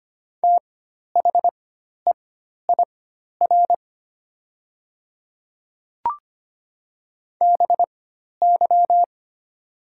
Morse code, heard as a keyed tone, about 25 wpm, spaced as for 12 wpm, 700 Hz. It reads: THEIR BY
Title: 4x Character Spacing - Top 100 Words - 25wpm